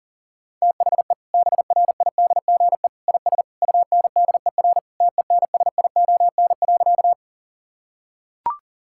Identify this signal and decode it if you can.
Morse code, heard as a keyed tone, about 40 wpm, 700 Hz. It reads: THE BRIDGE IS UNDER TENSION.
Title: Sentences from Top 1600 Words 40wpm